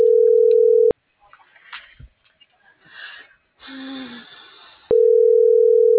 Ringback tone (North America), 440 + 480 Hz.